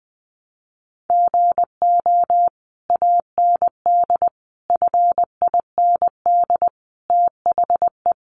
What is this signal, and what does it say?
Morse code, heard as a keyed tone, about 20 wpm, 700 Hz. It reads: GO AND FIND THE
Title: Sentences from Top 1200 Words 20wpm